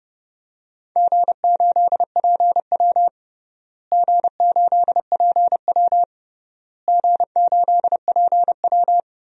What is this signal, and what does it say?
Morse code, heard as a keyed tone, about 30 wpm, 700 Hz. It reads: G8PW G8PW G8PW
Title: DXCC-100 Call Signs - 4 Characters - Repeated 3x - 30wpm